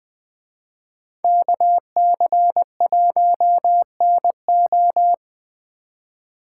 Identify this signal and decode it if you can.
Morse code, heard as a keyed tone, about 20 wpm, 700 Hz. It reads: KC1NO